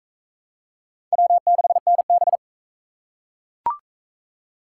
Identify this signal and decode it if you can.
Morse code, heard as a keyed tone, about 38 wpm, 700 Hz. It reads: W6NB